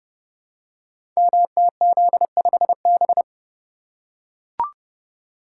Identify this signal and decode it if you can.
Morse code, heard as a keyed tone, about 30 wpm, 700 Hz. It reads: MTZ5B
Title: Code Groups - 5 Letter:Numbers - 30wpm